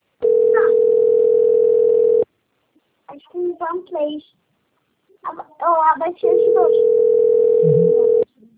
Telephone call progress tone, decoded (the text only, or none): ringback tone